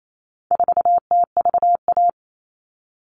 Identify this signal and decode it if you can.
Morse code, heard as a keyed tone, about 28 wpm, 700 Hz. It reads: 4TVA
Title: All Cut Numbers - 4-Digit 28wpm